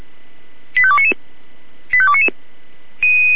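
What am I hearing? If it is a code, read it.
Five-tone selective call (ZVEI), heard as quick runs of five tones, tones of about 70 ms. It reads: 85299 85299
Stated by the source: ZVEI five-tone selective call received with SDR.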